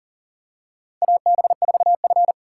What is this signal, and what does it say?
Morse code, heard as a keyed tone, about 37 wpm, 700 Hz. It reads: AB4F
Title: USA Call Signs - 4 Characters - 2x Character Spacing - 40wpm